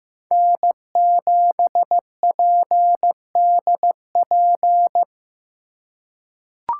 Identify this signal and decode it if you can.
Morse code, heard as a keyed tone, about 15 wpm, 700 Hz. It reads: N7PDP
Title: USA Call Signs - 5 Characters - 4x Character Spacing - 15wpm